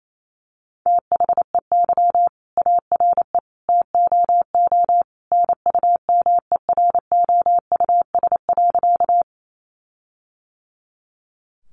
Morse code, heard as a keyed tone, about 28 wpm, 700 Hz. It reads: THEY ARE TOO NUMEROUS.
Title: Sentences from Top 1400 Words 28wpm